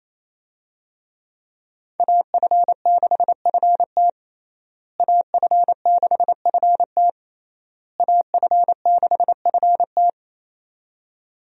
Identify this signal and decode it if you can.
Morse code, heard as a keyed tone, about 28 wpm, 700 Hz. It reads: AF6FT AF6FT AF6FT
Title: USA Call Signs - 5 Characters - Repeated 3x - 28wpm